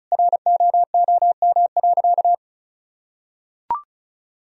Morse code, heard as a keyed tone, about 34 wpm, 700 Hz. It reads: ROOM.
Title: Sentences from Top 500 Words and Abbreviations 35wpm